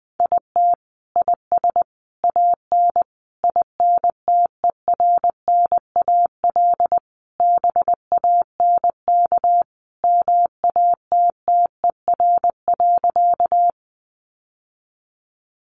Morse code, heard as a keyed tone, about 20 wpm, 700 Hz. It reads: IT IS AN INTERNAL BANK MATTER.
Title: Sentences from Top 1400 Words 20wpm